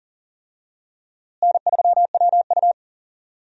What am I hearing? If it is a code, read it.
Morse code, heard as a keyed tone, about 40 wpm, 700 Hz. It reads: N3WU
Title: USA Call Signs - 4 Characters - Repeated 2x - 40wpm